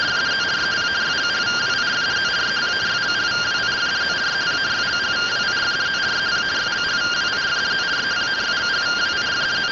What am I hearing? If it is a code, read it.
Two-tone teleprinter signal, about 170 Hz shift.
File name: findings / CT SAGITAL 27.gen-25 rtty rx